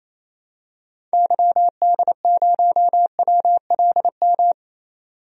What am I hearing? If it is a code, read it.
Morse code, heard as a keyed tone, about 27 wpm, 700 Hz. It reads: YD0WLM
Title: DXCC-100 Call Signs - 6 Characters - 28wpm